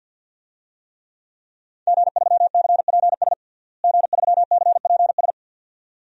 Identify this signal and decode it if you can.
Morse code, heard as a keyed tone, about 49 wpm, 700 Hz. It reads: G3CPS G3CPS
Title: DXCC-100 Call Signs - 5 Characters - Repeated 2x - 50wpm